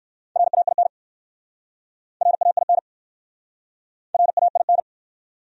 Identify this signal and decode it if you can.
Morse code, heard as a keyed tone, about 59 wpm, 700 Hz. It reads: RAIN RAIN RAIN